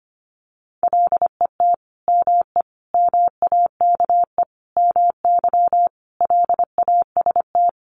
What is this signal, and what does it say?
Morse code, heard as a keyed tone, about 25 wpm, 700 Hz. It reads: LET ME MAKE MY LAST
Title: Sentences from Top 1300 Words 25wpm